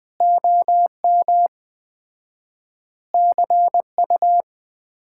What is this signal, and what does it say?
Morse code, heard as a keyed tone, about 20 wpm, 700 Hz. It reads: OM CU
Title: Common QSO Phrases - 4x Word-Spacing - 20wpm